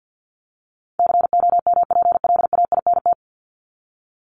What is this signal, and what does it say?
Morse code, heard as a keyed tone, about 50 wpm, 700 Hz. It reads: COMPLAINT
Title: Top 1251-1500 Words 50wpm